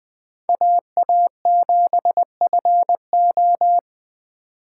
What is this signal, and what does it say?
Morse code, heard as a keyed tone, about 25 wpm, 700 Hz. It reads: AA7FO
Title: Speed Racing - USA 2x2 Call Signs 20wpm